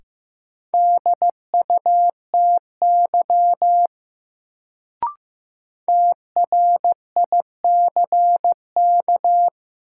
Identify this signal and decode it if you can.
Morse code, heard as a keyed tone, about 15 wpm, 700 Hz. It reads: DUTY TRICK
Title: Sentences from Top 1500 Words - New Word Drill - 15wpm